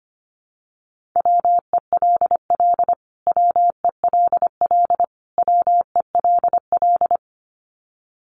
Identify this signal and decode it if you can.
Morse code, heard as a keyed tone, about 25 wpm, 700 Hz. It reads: WELL WELL WELL
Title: Top 500 Words Repeated 25wpm